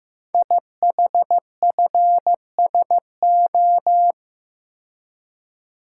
Morse code, heard as a keyed tone, about 15 wpm, 700 Hz. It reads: IHFSO
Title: Code Groups - 5 Letters - 15wpm